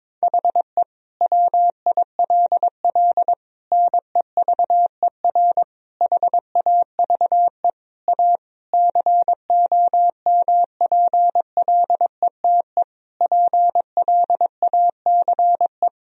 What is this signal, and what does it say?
Morse code, heard as a keyed tone, about 21 wpm, 700 Hz. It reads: HE WILL NEVER HAVE A COMPLETE PLACE
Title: Sentences from Top 1000 Words - Review Part 2 - 22wpm